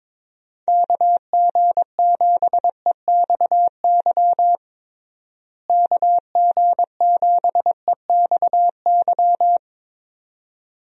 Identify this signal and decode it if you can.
Morse code, heard as a keyed tone, about 22 wpm, 700 Hz. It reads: KG7EXY KG7EXY
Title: USA Call Signs - 6 Characters - Repeated 2x - 22wpm